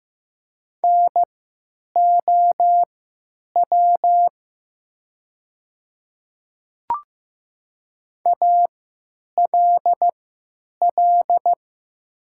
Morse code, heard as a keyed tone, about 15 wpm, spaced as for 9 wpm, 700 Hz. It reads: NOW ALL
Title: 3x Character Spacing - Top 500 Words - 15wpm